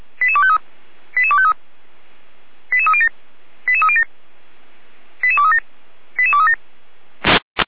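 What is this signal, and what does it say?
Five-tone selective call (ZVEI), heard as quick runs of five tones, tones of about 70 ms. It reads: 88253 88253 88307 88307 88237 88237
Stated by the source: ZVEI five-tone selective call 88209, received via SDR.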